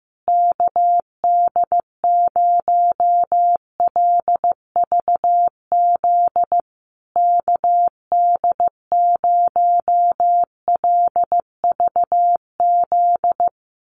Morse code, heard as a keyed tone, about 15 wpm, 700 Hz. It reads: KD0LVZ KD0LVZ
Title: USA Call Signs Repeated 3x 15wpm